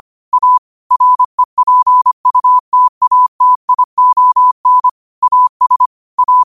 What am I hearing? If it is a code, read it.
Morse code, heard as a keyed tone, about 25 wpm, 1000 Hz. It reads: A REPUTATION AS A
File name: qotdfiles / 25wpm